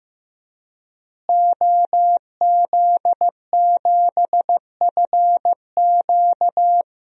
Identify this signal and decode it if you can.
Morse code, heard as a keyed tone, about 15 wpm, 700 Hz. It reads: OZ7FQ
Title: DXCC-100 Call Signs - 5 Characters - Repeated 2x - 15wpm